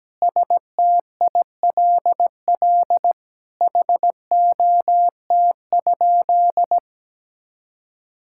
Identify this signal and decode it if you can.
Morse code, heard as a keyed tone, about 17 wpm, 700 Hz. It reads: STILL HOT?
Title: Sentences from Top 1000 Words - Review Part 6 - 17wpm